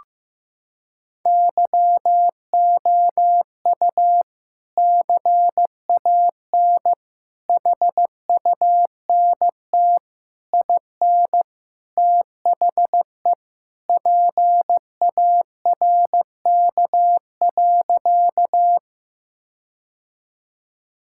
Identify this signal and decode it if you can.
Morse code, heard as a keyed tone, about 15 wpm, 700 Hz. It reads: YOU CAN HUNT IN THE PARK.
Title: Sentences from Top 1600 Words 15wpm